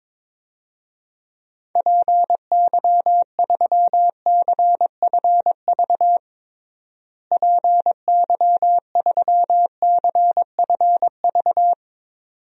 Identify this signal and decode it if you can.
Morse code, heard as a keyed tone, about 21 wpm, 700 Hz. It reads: PY3CFV PY3CFV
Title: DXCC-100 Call Signs - 6 Characters - Repeated 2x - 22wpm